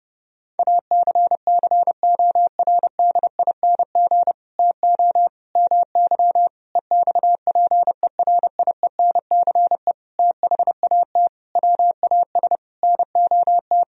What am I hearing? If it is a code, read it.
Morse code, heard as a keyed tone, about 30 wpm, 700 Hz. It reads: ACCORDING TO MY EXPERIENCE THAT WAS NOT
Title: Sentences from Top 1000 Words - Review Part 4 - 30wpm